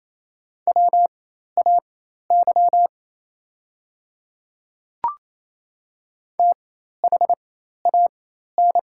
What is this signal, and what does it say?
Morse code, heard as a keyed tone, about 28 wpm, spaced as for 13 wpm, 690 Hz. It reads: WAY THAN
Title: 4x Character Spacing - Top 100 Words - 28wpm